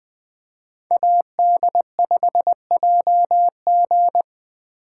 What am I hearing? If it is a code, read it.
Morse code, heard as a keyed tone, about 25 wpm, 700 Hz. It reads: AD5JG
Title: Speed Racing - USA 2x2 Call Signs 20wpm